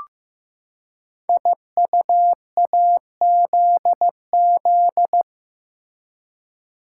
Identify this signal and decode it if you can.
Morse code, heard as a keyed tone, about 15 wpm, 700 Hz. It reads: IUAZZ
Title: Code Groups - 5 Letters - 15wpm